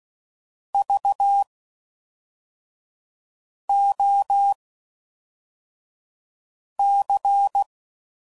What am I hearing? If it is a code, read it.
Morse code, heard as a keyed tone, about 16 wpm, spaced as for 4 wpm, 800 Hz. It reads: VOC